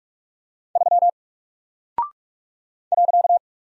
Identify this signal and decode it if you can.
Morse code, heard as a keyed tone, about 44 wpm, 700 Hz. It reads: 3 .
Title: Single Character (Letter:Number:Punctuation:Prosign) Alt 45wpm